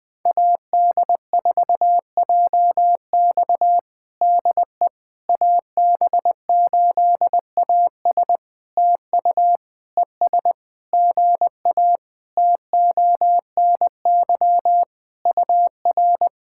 Morse code, heard as a keyed tone, about 20 wpm, 700 Hz. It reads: AD4JX DE AB8AS TU ES GA TONY UR
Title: POTA QSO Practice - 20wpm